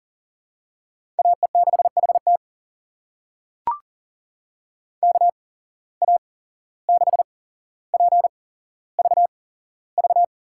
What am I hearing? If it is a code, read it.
Morse code, heard as a keyed tone, about 40 wpm, spaced as for 16 wpm, 700 Hz. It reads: AE6HT KA6PVV
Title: USA Call Signs - 8x Character Spacing - 40wpm